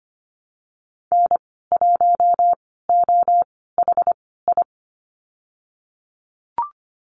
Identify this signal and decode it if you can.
Morse code, heard as a keyed tone, about 25 wpm, spaced as for 16 wpm, 690 Hz. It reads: N1OHI
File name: Lesson 15g - Number 5 - Call Signs 25wpm